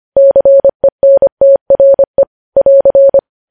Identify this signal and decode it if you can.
Morse code, heard as a keyed tone, about 25 wpm, 550 Hz. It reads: CENTRE <AR>
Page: news-headlines-in-morse-code-at-25-wpm